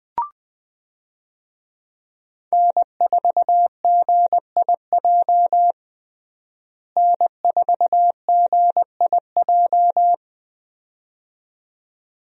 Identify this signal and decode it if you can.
Morse code, heard as a keyed tone, about 20 wpm, 700 Hz. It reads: N4GIJ N4GIJ
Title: USA Call Signs - 5 Characters - Repeated 2x - 20wpm